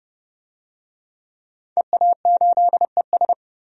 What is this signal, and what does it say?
Morse code, heard as a keyed tone, about 30 wpm, 700 Hz. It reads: EA8ES
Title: DXCC-100 Call Signs - 5 Characters - 30wpm